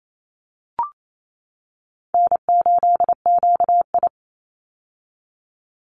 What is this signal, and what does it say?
Morse code, heard as a keyed tone, about 28 wpm, 700 Hz. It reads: N8QI
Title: USA Call Signs 28wpm